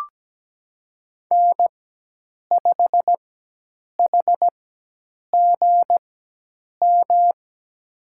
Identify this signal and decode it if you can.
Morse code, heard as a keyed tone, about 17 wpm, spaced as for 8 wpm, 700 Hz. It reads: N5HGM
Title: USA Call Signs - 5 Characters - 4x Character Spacing - 17wpm